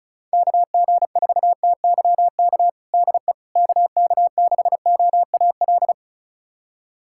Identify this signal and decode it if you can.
Morse code, heard as a keyed tone, about 34 wpm, 690 Hz. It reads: KG4TYK DE KK6OAL
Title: USA Callsign DE Callsign - 6 Characters - 35wpm